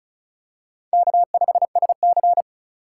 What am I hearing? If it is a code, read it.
Morse code, heard as a keyed tone, about 34 wpm, 700 Hz. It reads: K5SC
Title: USA Call Signs - 4 Characters - 4x Character Spacing - 35wpm